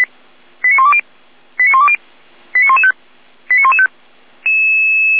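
Five-tone selective call (ZVEI), heard as quick runs of five tones, tones of about 70 ms. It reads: 89120 89120 89115 89115
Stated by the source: ZVEI five-tone call received by SDR.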